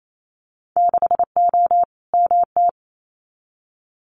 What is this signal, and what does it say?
Morse code, heard as a keyed tone, about 28 wpm, 700 Hz. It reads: 6O MT
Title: Winter Field Day Exchanges Encore 28wpm